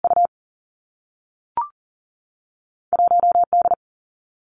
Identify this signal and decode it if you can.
Morse code, heard as a keyed tone, about 42 wpm, 700 Hz. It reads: U 1 D
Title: Two Challenging Letter-Number Combinations 40wpm